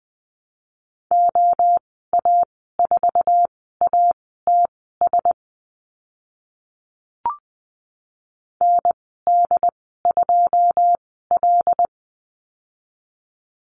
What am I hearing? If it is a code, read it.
Morse code, heard as a keyed tone, about 20 wpm, spaced as for 14 wpm, 700 Hz. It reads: OA4ATS ND2L
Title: 082 Lesson 14g - Number 2 - Call Signs 20wpm